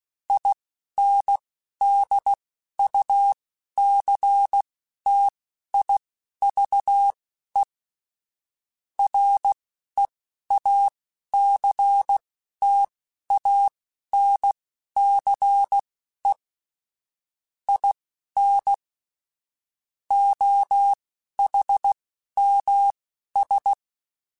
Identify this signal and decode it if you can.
Morse code, heard as a keyed tone, about 16 wpm, spaced as for 11 wpm, 800 Hz. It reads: INDUCTIVE REACTANCE IN OHMS
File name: Technical Subjects / Technical Subjects-mp3s / XL @10